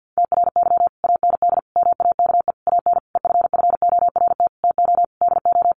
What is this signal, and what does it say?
Morse code, heard as a keyed tone, about 50 wpm, 700 Hz. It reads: TRY AND MAKE AN EFFORT TO DO
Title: Sentences from Top 1200 Words 50wpm